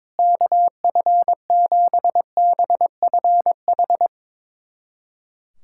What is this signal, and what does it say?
Morse code, heard as a keyed tone, about 21 wpm, 700 Hz. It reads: KF7BFH
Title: USA Callsign DE Callsign 22wpm